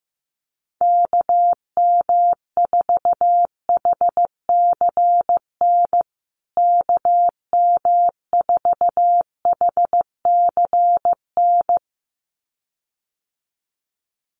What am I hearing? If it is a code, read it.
Morse code, heard as a keyed tone, about 15 wpm, 700 Hz. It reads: KM4HCN KM4HCN
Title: USA Call Signs Repeated 2x 15wpm